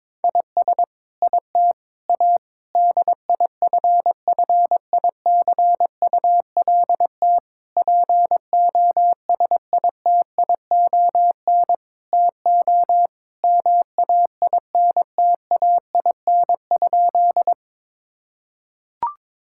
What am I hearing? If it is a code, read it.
Morse code, heard as a keyed tone, about 22 wpm, 700 Hz. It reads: IS IT A DIFFICULT POSITION TO MAINTAIN?
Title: Sentences from Top 1000 Words - Review Part 5 - 22wpm